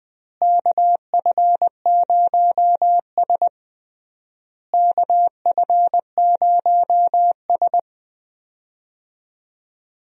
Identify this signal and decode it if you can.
Morse code, heard as a keyed tone, about 20 wpm, 700 Hz. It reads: KF0S KF0S
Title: USA Call Signs - 4 Characters - Repeated 3x - 20wpm